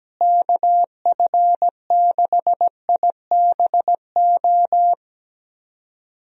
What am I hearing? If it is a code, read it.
Morse code, heard as a keyed tone, about 17 wpm, 700 Hz. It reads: KF6IBO
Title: USA Call Signs - 6 Characters - Repeated 2x - 17wpm